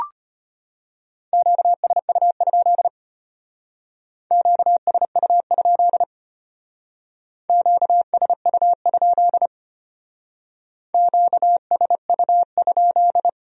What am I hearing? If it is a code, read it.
Morse code, heard as a keyed tone, about 30 wpm, 700 Hz. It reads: QSU? QSU? QSU? QSU?
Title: Speed-Racing Q-Codes - 25wpm